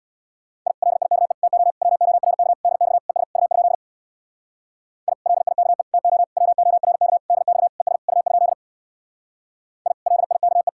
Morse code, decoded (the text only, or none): I LIKE TO WORK DX ES U? I LIKE TO WORK DX ES U? I LIKE